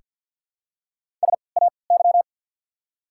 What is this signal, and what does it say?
Morse code, heard as a keyed tone, about 48 wpm, 690 Hz. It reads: S A Y
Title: 3x Character Spacing - Top 300 Words - 50wpm